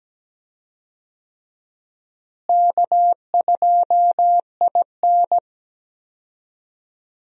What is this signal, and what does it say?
Morse code, heard as a keyed tone, about 17 wpm, 700 Hz. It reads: K2IN
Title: USA Call Signs - 4 Characters - Repeated 2x - 17wpm